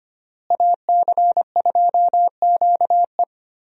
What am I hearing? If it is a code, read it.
Morse code, heard as a keyed tone, about 25 wpm, 700 Hz. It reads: AC2QE
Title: USA Call Signs - 5 Characters - Repeated 2x - 25wpm